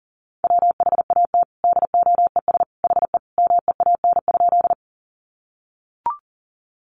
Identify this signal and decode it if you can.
Morse code, heard as a keyed tone, about 37 wpm, 700 Hz. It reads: WHAT DOES HE MEAN?
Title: Sentences from Top 200 Words 40wpm